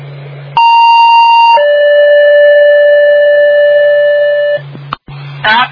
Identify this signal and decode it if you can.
Two-tone page, heard as a one-second tone, then a three-second tone, about 930 then 580 Hz, with noise. It is heard from 0.6 to 4.6 s.